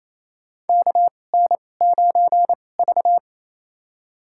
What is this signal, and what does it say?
Morse code, heard as a keyed tone, about 27 wpm, spaced as for 20 wpm, 700 Hz. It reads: KN9V
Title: USA Call Signs - 4 Characters - 2x Character Spacing - 28wpm